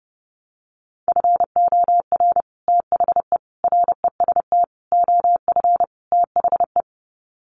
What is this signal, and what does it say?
Morse code, heard as a keyed tone, about 30 wpm, 700 Hz. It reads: FOR THE REST OF THE
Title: Sets of 5 Words 30wpm